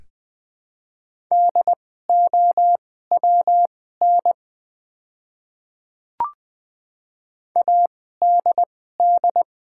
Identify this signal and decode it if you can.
Morse code, heard as a keyed tone, about 20 wpm, spaced as for 15 wpm, 700 Hz. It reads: DOWN ADD
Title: 2x Character Spacing - Top 500 Words - 20wpm